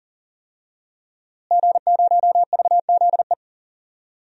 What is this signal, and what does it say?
Morse code, heard as a keyed tone, about 40 wpm, 700 Hz. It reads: G0VZE
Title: DXCC-100 Call Signs - 5 Characters - Repeated 3x - 40wpm